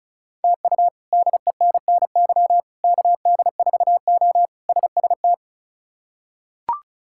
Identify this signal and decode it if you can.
Morse code, heard as a keyed tone, about 34 wpm, 700 Hz. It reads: TU DENNY KD4O SST
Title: K1USN SST QSO Practice - 35wpm